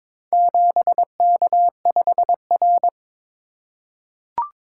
Morse code, heard as a keyed tone, about 22 wpm, 700 Hz. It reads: 7K5R
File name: practice / 22wpm / Four Letter-Number Combinations - 22wpm